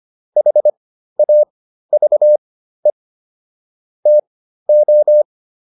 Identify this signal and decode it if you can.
Morse code, heard as a keyed tone, about 25 wpm, spaced as for 13 wpm, 600 Hz. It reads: HAVE TO